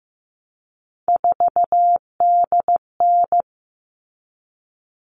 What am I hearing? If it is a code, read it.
Morse code, heard as a keyed tone, about 15 wpm, 700 Hz. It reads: 4DN
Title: All Cut Numbers - 3-Digit 15wpm